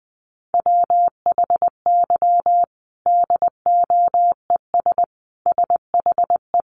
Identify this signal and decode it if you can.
Morse code, heard as a keyed tone, about 20 wpm, 700 Hz. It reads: WHY DOES SHE
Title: Sentences from Top 1300 Words 20wpm